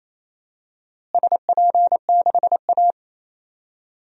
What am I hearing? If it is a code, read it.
Morse code, heard as a keyed tone, about 28 wpm, 700 Hz. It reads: SP6A
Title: DXCC-100 Call Signs - 4 Characters - Repeated 2x - 28wpm